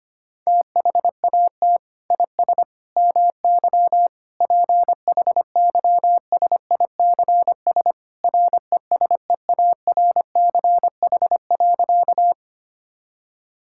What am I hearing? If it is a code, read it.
Morse code, heard as a keyed tone, about 25 wpm, 700 Hz. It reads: THAT IS MY PHYSICS RESEARCH.
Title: Sentences from Top 1000 Words - Review Part 5 - 25wpm